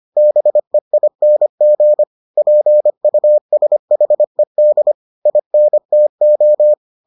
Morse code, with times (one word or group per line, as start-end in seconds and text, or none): 0.2-2.0 BEING
2.4-4.9 PUSHED
5.3-6.7 INTO